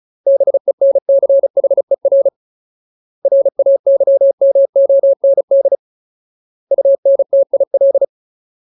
Morse code, heard as a keyed tone, about 34 wpm, 550 Hz. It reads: BENCHER RAYMOND UNTIL
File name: Ultimate Word List / 35WPM Ultimate